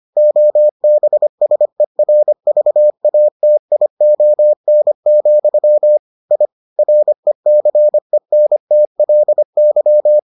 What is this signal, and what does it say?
Morse code, heard as a keyed tone, about 25 wpm, 600 Hz. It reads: OBSERVATION, I RECENTLY